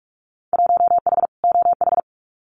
Morse code, heard as a keyed tone, about 46 wpm, 700 Hz. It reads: 1H OH
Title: Winter Field Day Exchanges 45wpm